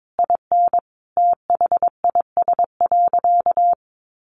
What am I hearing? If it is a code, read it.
Morse code, heard as a keyed tone, about 22 wpm, 700 Hz. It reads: IN THIS.
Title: Sentences from Top 1200 Words 22wpm